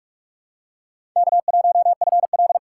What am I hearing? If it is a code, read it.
Morse code, heard as a keyed tone, about 46 wpm, 700 Hz. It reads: K1FL
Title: USA Call Signs - 4x Character Spacing - 45wpm